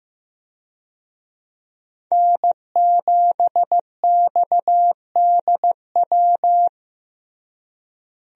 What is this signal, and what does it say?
Morse code, heard as a keyed tone, about 15 wpm, 700 Hz. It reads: N7XDW
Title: USA Call Signs - 5 Characters - Repeated 3x - 15wpm